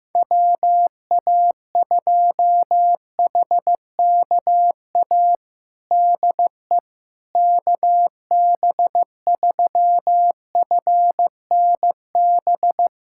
Morse code, heard as a keyed tone, about 15 wpm, 700 Hz. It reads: WA2HKA DE KB3FNB
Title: USA Callsign DE Callsign - 6 Characters - 15wpm